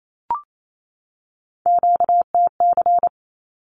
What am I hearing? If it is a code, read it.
Morse code, heard as a keyed tone, about 28 wpm, 700 Hz. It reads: QTC
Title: Q-Codes 28wpm